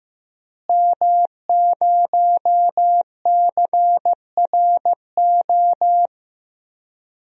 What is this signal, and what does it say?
Morse code, heard as a keyed tone, about 15 wpm, 700 Hz. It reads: M0CRO
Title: DXCC-100 Call Signs - 5 Characters - Repeated 2x - 15wpm